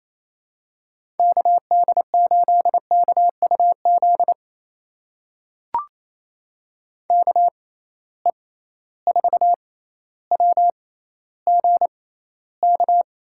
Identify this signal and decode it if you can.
Morse code, heard as a keyed tone, about 28 wpm, spaced as for 14 wpm, 700 Hz. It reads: KD8KUZ KE4WGK
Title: USA Call Signs - 6 Characters - 6x Character Spacing - 28wpm